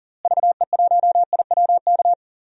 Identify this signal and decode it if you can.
Morse code, heard as a keyed tone, about 40 wpm, 700 Hz. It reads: VE1IWK
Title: DXCC-100 Call Signs - 6 Characters - Repeated 2x - 40wpm